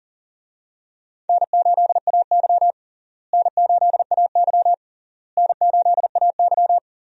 Morse code, heard as a keyed tone, about 40 wpm, 700 Hz. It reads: N8AY N8AY N8AY
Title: USA Call Signs - 4 Characters - Repeated 3x - 40wpm